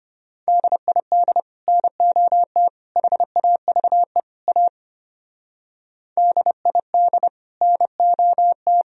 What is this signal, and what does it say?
Morse code, heard as a keyed tone, about 33 wpm, 700 Hz. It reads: DID NOT HAVE A DID NOT